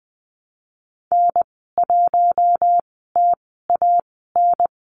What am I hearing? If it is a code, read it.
Morse code, heard as a keyed tone, about 20 wpm, spaced as for 14 wpm, 700 Hz. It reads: N1TAN
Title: 062 Lesson 11g - Letter h - Call Signs 20wpm